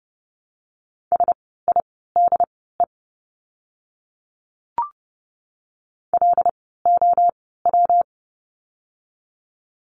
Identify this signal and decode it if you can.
Morse code, heard as a keyed tone, about 30 wpm, spaced as for 17 wpm, 700 Hz. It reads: SIDE LOW
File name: course / 30:10wpm / 115 Lesson 19f - Letter w - Words 30wpm